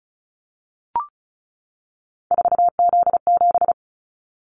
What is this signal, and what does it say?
Morse code, heard as a keyed tone, about 34 wpm, 700 Hz. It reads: 4Z7
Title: Callsign Prefixes for Top 50 DXCC - 35wpm